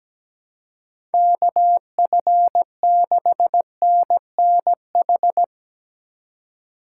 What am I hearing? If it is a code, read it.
Morse code, heard as a keyed tone, about 18 wpm, 700 Hz. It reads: KF6NNH
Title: Speed Racing - USA 2x3 Call Signs 17wpm